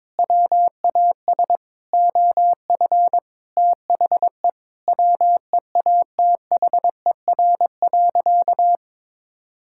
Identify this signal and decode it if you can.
Morse code, heard as a keyed tone, about 22 wpm, 700 Hz. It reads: WAS OF THE WEATHER.
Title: Sentences from Top 1000 Words - Review Part 5 - 22wpm